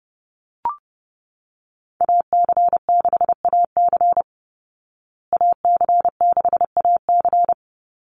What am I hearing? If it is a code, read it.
Morse code, heard as a keyed tone, about 30 wpm, 700 Hz. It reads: AC6AC AC6AC
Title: Call Signs Repeated 3x - 30wpm